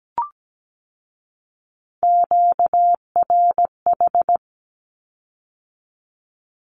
Morse code, heard as a keyed tone, about 17 wpm, 700 Hz. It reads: QRH
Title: Q-Codes with Spoken Meaning - 17wpm